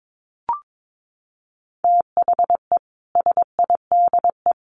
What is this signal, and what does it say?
Morse code, heard as a keyed tone, about 22 wpm, 700 Hz. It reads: THE SIDE
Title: Sets of 4 Words 22wpm